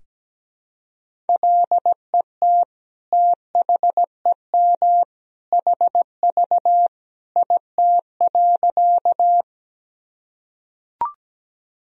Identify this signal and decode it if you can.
Morse code, heard as a keyed tone, about 17 wpm, 700 Hz. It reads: LET THEM HV IT.
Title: Sentences from Top 300 Words and Abbreviations 17wpm